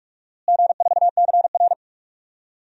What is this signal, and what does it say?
Morse code, heard as a keyed tone, about 46 wpm, 700 Hz. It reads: G4CR
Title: DXCC-100 Call Signs - 4 Characters - Repeated 3x - 45wpm